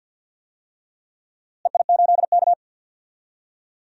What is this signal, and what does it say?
Morse code, heard as a keyed tone, about 46 wpm, 700 Hz. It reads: EI8K
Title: DXCC-100 Call Signs - 4 Characters - Repeated 3x - 50wpm